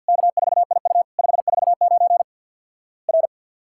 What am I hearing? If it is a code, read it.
Morse code, heard as a keyed tone, about 50 wpm, 700 Hz. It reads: K4IU 549 R
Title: Worked All Europe DX Contest - QTC Exchange with Repeats - 50wpm